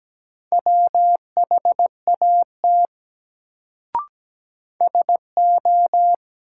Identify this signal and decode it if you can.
Morse code, heard as a keyed tone, about 17 wpm, 700 Hz. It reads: WHAT SO
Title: Sentences from Top 100 Words - New Word Drill - 17wpm